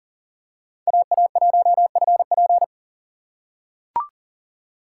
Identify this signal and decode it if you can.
Morse code, heard as a keyed tone, about 40 wpm, 700 Hz. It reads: AA1FP
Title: USA Call Signs - 5 Characters - 4x Character Spacing - 40wpm